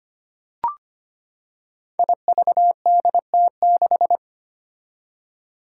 Morse code, heard as a keyed tone, about 25 wpm, 700 Hz. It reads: IVDT6
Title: Five Challenging Letter-Number Combinations 25wpm